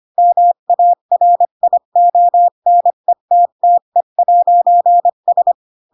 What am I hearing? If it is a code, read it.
Morse code, heard as a keyed tone, about 25 wpm, 700 Hz. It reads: MARIONETTE'S
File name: Pinocchio / Pinocchio ch5 20 25 ik1hge